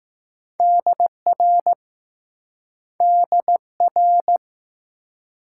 Morse code, heard as a keyed tone, about 19 wpm, 700 Hz. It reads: DR DR